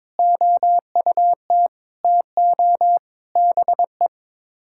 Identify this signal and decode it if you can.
Morse code, heard as a keyed tone, about 22 wpm, 700 Hz. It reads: OUT TO BE
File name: practice / 22wpm / Build A Phrase - 5-Word Phrases - 22wpm